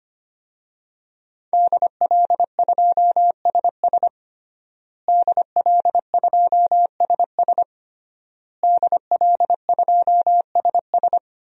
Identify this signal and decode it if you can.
Morse code, heard as a keyed tone, about 25 wpm, 700 Hz. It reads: DL2SS DL2SS DL2SS
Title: DXCC-100 Call Signs - 5 Characters - Repeated 3x - 25wpm